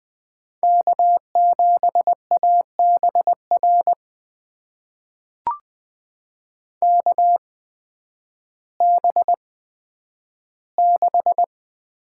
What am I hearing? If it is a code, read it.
Morse code, heard as a keyed tone, about 20 wpm, spaced as for 11 wpm, 700 Hz. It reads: K7ABR KB6
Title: USA Call Signs - 8x Character Spacing - 20wpm